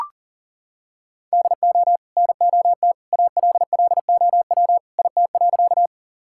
Morse code, heard as a keyed tone, about 40 wpm, 700 Hz. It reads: DO NOT ALLOW IT.
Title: Sentences from Top 1000 Words - Review Part 5 - 40wpm